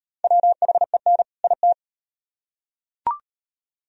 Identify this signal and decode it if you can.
Morse code, heard as a keyed tone, about 34 wpm, 700 Hz. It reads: WHEN IT